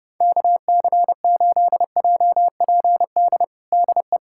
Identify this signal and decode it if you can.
Morse code, heard as a keyed tone, about 30 wpm, 700 Hz. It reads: KC8JPD DE
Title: USA Callsign DE Callsign - 6 Characters - 30wpm